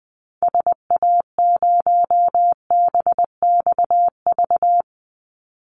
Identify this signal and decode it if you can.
Morse code, heard as a keyed tone, about 20 wpm, 700 Hz. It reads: SA0BXV
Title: Call Signs Repeated 3x - 20wpm